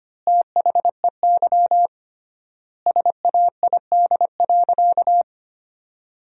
Morse code, heard as a keyed tone, about 25 wpm, 700 Hz. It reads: THEY SAID.
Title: 3x Word-Spacing - Sentences from Top 100 Words - 25wpm